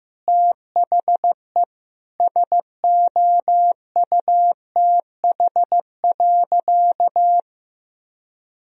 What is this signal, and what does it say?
Morse code, heard as a keyed tone, about 15 wpm, 700 Hz. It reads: THE SOUTH.
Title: Sentences from Top 1000 Words - Review Part 2 - 15wpm